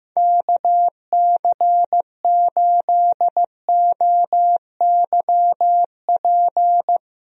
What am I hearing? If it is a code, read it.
Morse code, heard as a keyed tone, about 15 wpm, 700 Hz. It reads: KC8OYP
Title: USA Call Signs - 6 Characters - Repeated 2x - 15wpm